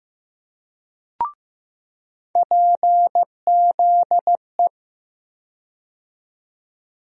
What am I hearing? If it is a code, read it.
Morse code, heard as a keyed tone, about 15 wpm, 700 Hz. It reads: PZE
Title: USA Call Sign Postfixes - 3 Letters 15wpm